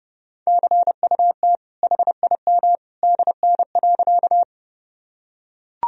Morse code, heard as a keyed tone, about 30 wpm, 700 Hz. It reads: CUT HIM DN.
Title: Sentences from Top 300 Words and Abbreviations 30wpm